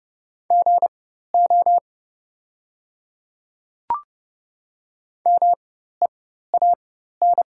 Morse code, heard as a keyed tone, about 30 wpm, spaced as for 14 wpm, 700 Hz. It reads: GO MEAN
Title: 4x Character Spacing - Top 300 Words - 30wpm